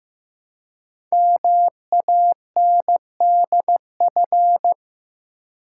Morse code, heard as a keyed tone, about 15 wpm, 700 Hz. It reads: MANDF
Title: Code Groups - 5 Letters - 15wpm